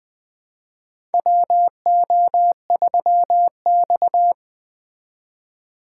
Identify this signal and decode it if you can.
Morse code, heard as a keyed tone, about 20 wpm, 700 Hz. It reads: WO3X
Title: USA Call Signs - 4 Characters - 8x Character Spacing - 20wpm